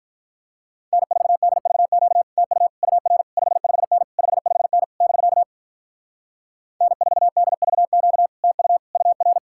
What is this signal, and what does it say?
Morse code, heard as a keyed tone, about 47 wpm, 700 Hz. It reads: N4DVQ TU UR 55N 55N <BK> N4DVQ TU UR